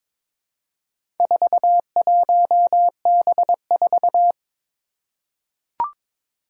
Morse code, heard as a keyed tone, about 22 wpm, 700 Hz. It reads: 41B4
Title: Four Challenging Letter-Number Combinations 22wpm